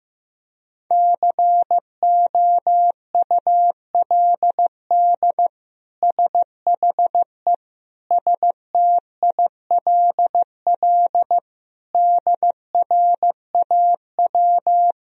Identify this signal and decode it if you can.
Morse code, heard as a keyed tone, about 15 wpm, 700 Hz. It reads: COULD SHE STILL DRAW
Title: Sentences from Top 1000 Words - Review Part 1 - 15wpm